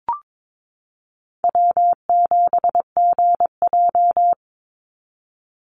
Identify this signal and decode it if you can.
Morse code, heard as a keyed tone, about 22 wpm, 700 Hz. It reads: W7GJ
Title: NA SOTA Chaser Callsigns 22wpm